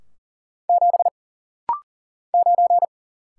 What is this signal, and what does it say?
Morse code, heard as a keyed tone, about 40 wpm, 700 Hz. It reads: 7 9
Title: Single Letter-Number - Rapid-Fire 40wpm